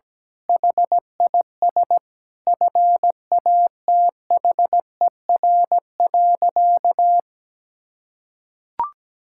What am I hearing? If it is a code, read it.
Morse code, heard as a keyed tone, about 17 wpm, 700 Hz. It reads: HIS FATHER.
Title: Sentences from Top 1000 Words - Review Part 5 - 17wpm